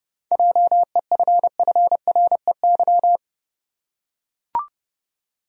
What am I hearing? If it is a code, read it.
Morse code, heard as a keyed tone, about 31 wpm, 700 Hz. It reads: JEFFREY
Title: USA Common Names - 30wpm